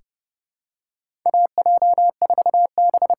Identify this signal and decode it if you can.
Morse code, heard as a keyed tone, about 29 wpm, 700 Hz. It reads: AJ4B
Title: USA Call Signs - 5 Characters - 8x Character Spacing - 30wpm